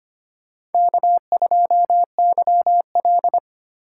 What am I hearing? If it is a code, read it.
Morse code, heard as a keyed tone, about 25 wpm, 700 Hz. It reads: K2YL
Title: USA Call Signs - 4 Characters - 4x Character Spacing - 25wpm